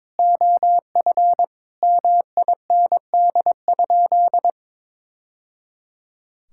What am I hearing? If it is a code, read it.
Morse code, heard as a keyed tone, about 22 wpm, 700 Hz. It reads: OF MIND?
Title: Sentences from Top 1000 Words - Review Part 5 - 22wpm